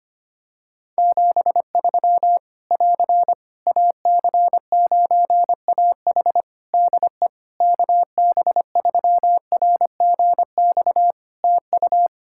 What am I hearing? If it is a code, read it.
Morse code, heard as a keyed tone, about 25 wpm, 700 Hz. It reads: 73 <AR> AC9AH DE KB3RGX TU
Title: Practice QSO - Exchange 4 - Ending - 25wpm